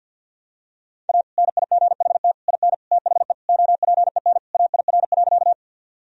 Morse code, heard as a keyed tone, about 52 wpm, 700 Hz. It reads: A NIGHT IN THE OPEN AIR.
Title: Sentences from Top 1000 Words - Review Part 1 - 50wpm